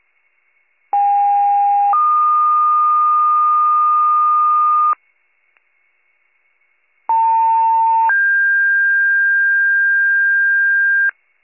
Two-tone page, heard as a one-second tone, then a three-second tone, about 810 then 1220 Hz, about 900 then 1600 Hz. It is heard from 0.9 to 11.1 s.